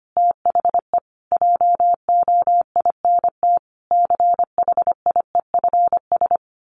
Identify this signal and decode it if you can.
Morse code, heard as a keyed tone, about 25 wpm, 700 Hz. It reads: THE JOINT CHIEFS
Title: Sets of 5 Words Encore 25wpm